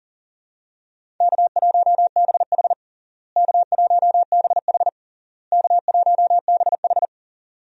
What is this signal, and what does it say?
Morse code, heard as a keyed tone, about 40 wpm, 700 Hz. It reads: K1BH K1BH K1BH
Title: USA Call Signs - 4 Characters - Repeated 3x - 40wpm